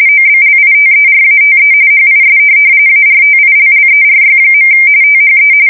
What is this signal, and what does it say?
RTTY, about 45.45 baud. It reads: OBLITERATE HIS ARTILLERY. THE
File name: fsk170 / USS-New-Jersey